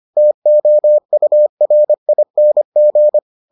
Morse code, heard as a keyed tone, about 25 wpm, 600 Hz. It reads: TOURING